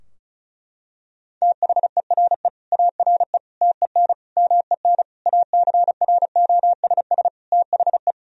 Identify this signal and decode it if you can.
Morse code, heard as a keyed tone, about 36 wpm, 700 Hz. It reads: THERE ARE TEN MEN ACROSS THE
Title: Sentences from Top 1000 Words - Review Part 2 - 35wpm